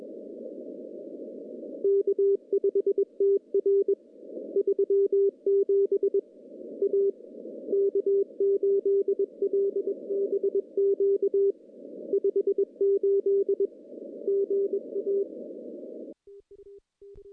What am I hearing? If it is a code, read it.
Morse code, heard as a keyed tone, about 22 wpm, 410 Hz. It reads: K5TR 37 A K8LBQ 58 GA
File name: contests / CW-SS-k5tr- -0300